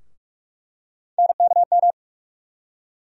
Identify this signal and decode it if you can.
Morse code, heard as a keyed tone, about 46 wpm, 700 Hz. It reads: NKM
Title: USA Call Sign Postfixes - 3 Letters 45wpm